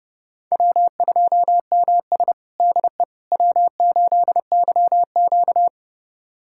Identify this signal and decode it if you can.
Morse code, heard as a keyed tone, about 32 wpm, 700 Hz. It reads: W2MS DE W8YQ
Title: USA Callsign DE Callsign - 4 Characters - 30wpm